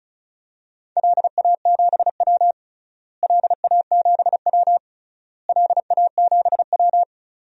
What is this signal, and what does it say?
Morse code, heard as a keyed tone, about 36 wpm, 700 Hz. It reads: LA7W LA7W LA7W
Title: DXCC-100 Call Signs - 4 Characters - Repeated 3x - 35wpm